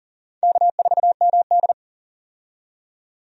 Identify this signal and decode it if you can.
Morse code, heard as a keyed tone, about 40 wpm, 700 Hz. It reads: K4MD
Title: USA Call Signs - 4 Characters - 40wpm